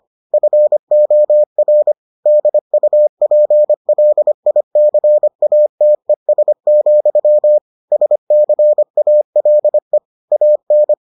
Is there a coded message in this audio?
Yes: Morse code from 0.3 to 10.9 s, heard as a keyed tone, about 25 wpm, 600 Hz. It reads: FOR DUPLICATES, SCALE AN